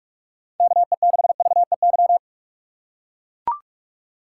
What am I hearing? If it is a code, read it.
Morse code, heard as a keyed tone, about 44 wpm, 700 Hz. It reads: KE6VEY
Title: USA Call Signs - 2x Character Spacing - 45wpm